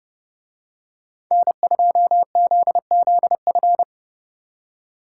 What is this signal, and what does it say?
Morse code, heard as a keyed tone, about 28 wpm, 700 Hz. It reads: N2ZZF